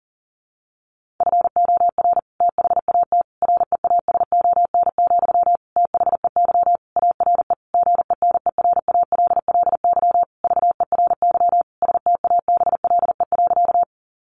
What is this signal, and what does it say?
Morse code, heard as a keyed tone, about 40 wpm, 700 Hz. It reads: FOR THAT REASON, THEY ARE GENERALLY VERY STABLE.
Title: Sentences from Top 1500 Words 40wpm